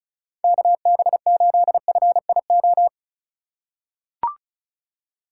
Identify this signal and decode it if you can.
Morse code, heard as a keyed tone, about 36 wpm, 700 Hz. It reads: KB8FIO
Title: USA Call Signs - 2x Character Spacing - 35wpm